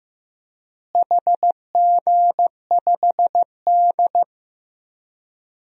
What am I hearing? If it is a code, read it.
Morse code, heard as a keyed tone, about 15 wpm, 700 Hz. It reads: HG5D
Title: DXCC-100 Call Signs - 4 Characters - Repeated 2x - 15wpm